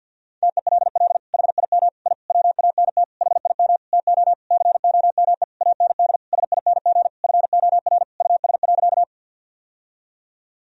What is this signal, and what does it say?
Morse code, heard as a keyed tone, about 50 wpm, 700 Hz. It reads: TELL HIM I WANT HIM TO COME AND SING FOR US.
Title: Sentences from Top 1000 Words - Review Part 2 - 50wpm